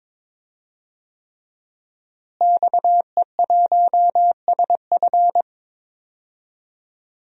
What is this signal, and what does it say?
Morse code, heard as a keyed tone, about 22 wpm, 700 Hz. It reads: XE1SF
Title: DXCC-100 Call Signs - 5 Characters - Repeated 3x - 22wpm